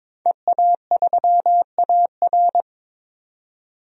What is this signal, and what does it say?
Morse code, heard as a keyed tone, about 22 wpm, 700 Hz. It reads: EA3AR
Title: DXCC-100 Call Signs - 5 Characters - 22wpm